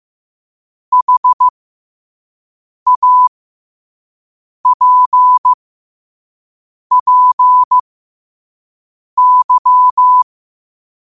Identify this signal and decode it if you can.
Morse code, heard as a keyed tone, about 15 wpm, spaced as for 5 wpm, 1000 Hz. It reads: HAPPY
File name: qotdfiles / 5wpm